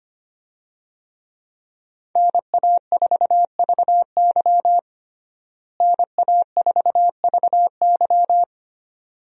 Morse code, heard as a keyed tone, about 25 wpm, 700 Hz. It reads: NA4VY NA4VY
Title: USA Call Signs - 5 Characters - Repeated 2x - 25wpm